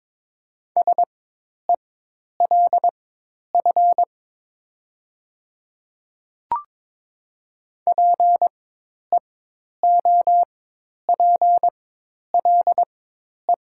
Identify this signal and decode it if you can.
Morse code, heard as a keyed tone, about 22 wpm, spaced as for 10 wpm, 700 Hz. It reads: SELF PEOPLE
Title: 4x Character Spacing - Top 500 Words - 22wpm